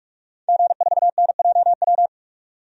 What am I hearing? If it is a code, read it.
Morse code, heard as a keyed tone, about 46 wpm, 690 Hz. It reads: G4NJW
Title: DXCC-100 Call Signs - 5 Characters - 45wpm